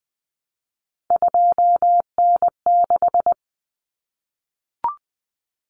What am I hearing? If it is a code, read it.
Morse code, heard as a keyed tone, about 20 wpm, 700 Hz. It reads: 2N6
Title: Common Cut Numbers - 3-Digit 20wpm